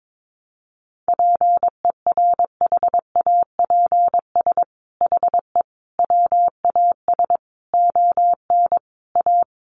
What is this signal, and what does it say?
Morse code, heard as a keyed tone, about 22 wpm, 700 Hz. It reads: PERHAPS HE WAS ON A
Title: Sentences from Top 1100 Words 22wpm